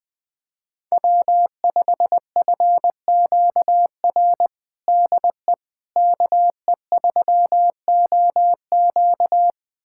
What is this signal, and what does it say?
Morse code, heard as a keyed tone, about 20 wpm, 700 Hz. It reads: W5FQR DE KE3OQ
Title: USA Callsign DE Callsign - 5 Characters - 20wpm